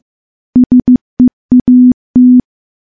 Morse code, heard as a keyed tone, about 14 wpm, 250 Hz. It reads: SEAT
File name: morse-training-15WPM